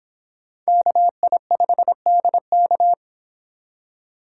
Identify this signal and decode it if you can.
Morse code, heard as a keyed tone, about 24 wpm, 700 Hz. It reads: KI5DK